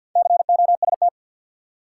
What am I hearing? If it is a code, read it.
Morse code, heard as a keyed tone, about 52 wpm, 700 Hz. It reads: COST
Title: Four Letter Words - 50wpm